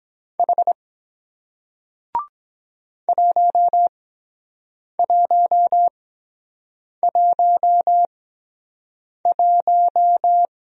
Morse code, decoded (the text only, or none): H 1111